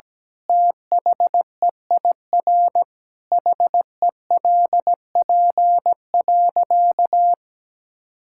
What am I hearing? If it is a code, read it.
Morse code, heard as a keyed tone, about 17 wpm, 700 Hz. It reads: THEIR HELP.
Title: Sentences from Top 1000 Words - Review Part 6 - 17wpm